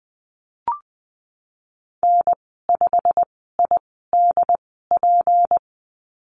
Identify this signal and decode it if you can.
Morse code, heard as a keyed tone, about 20 wpm, spaced as for 15 wpm, 700 Hz. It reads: N5IDP
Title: 167 Lesson 27g - Number 7 - Call Signs 20wpm